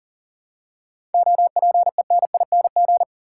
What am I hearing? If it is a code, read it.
Morse code, heard as a keyed tone, about 40 wpm, 700 Hz. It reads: OPENING
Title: Seven Letter Words - 40wpm